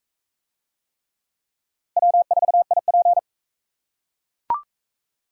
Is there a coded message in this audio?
Yes: Morse code from 2.0 to 3.2 s, heard as a keyed tone, about 37 wpm, 700 Hz. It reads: W4IP